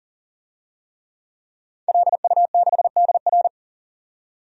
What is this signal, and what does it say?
Morse code, heard as a keyed tone, about 40 wpm, 700 Hz. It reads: LU6DR
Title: DXCC-100 Call Signs - 5 Characters - 40wpm